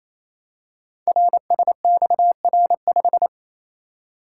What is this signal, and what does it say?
Morse code, heard as a keyed tone, about 27 wpm, 700 Hz. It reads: RSXR5
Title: Code Groups - 5 Letter:Numbers - 28wpm